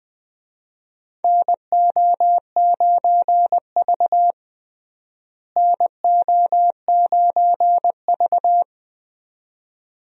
Morse code, heard as a keyed tone, about 20 wpm, 700 Hz. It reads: NO9V NO9V
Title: USA Call Signs - 4 Characters - Repeated 2x - 20wpm